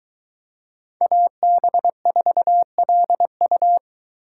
Morse code, heard as a keyed tone, about 24 wpm, 700 Hz. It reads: AB4LU